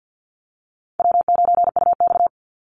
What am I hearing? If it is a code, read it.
Morse code, heard as a keyed tone, about 50 wpm, 700 Hz. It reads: W9UX
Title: NA SOTA Chaser Callsigns 50wpm